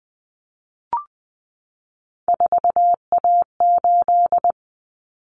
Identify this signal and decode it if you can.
Morse code, heard as a keyed tone, about 20 wpm, 700 Hz. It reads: 4A8
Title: Common Cut Numbers - 3-Digit 20wpm